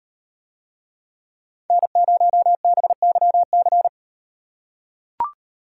Morse code, heard as a keyed tone, about 35 wpm, 700 Hz. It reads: N0BYC